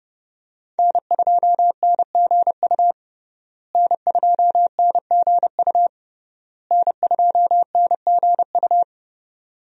Morse code, heard as a keyed tone, about 29 wpm, 700 Hz. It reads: N2NGU N2NGU N2NGU
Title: USA Call Signs - 5 Characters - Repeated 3x - 30wpm